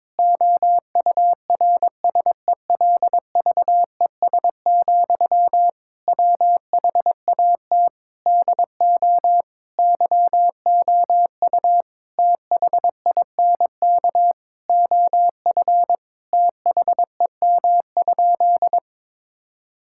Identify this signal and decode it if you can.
Morse code, heard as a keyed tone, about 22 wpm, 700 Hz. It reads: OURSELVES, WHAT DO YOU THINK OF THEM?
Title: Sentences from Top 1600 Words 22wpm